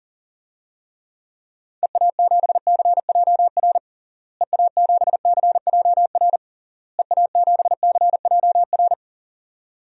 Morse code, heard as a keyed tone, about 40 wpm, 700 Hz. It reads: EA7CJR EA7CJR EA7CJR
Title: DXCC-100 Call Signs - 6 Characters - Repeated 3x - 40wpm